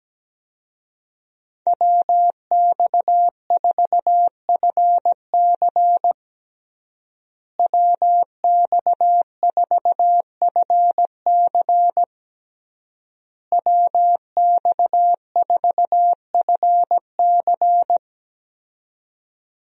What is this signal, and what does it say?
Morse code, heard as a keyed tone, about 17 wpm, 700 Hz. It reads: WX4FC WX4FC WX4FC
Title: USA Call Signs - 5 Characters - Repeated 3x - 17wpm